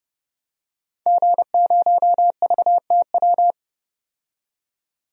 Morse code, heard as a keyed tone, about 32 wpm, 700 Hz. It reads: G0VTW
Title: DXCC-100 Call Signs - 5 Characters - 30wpm